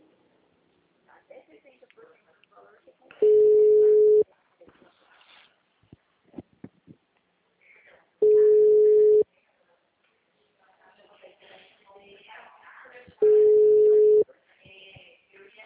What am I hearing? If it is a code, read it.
Ringback tone (Europe, 425 Hz), with speech.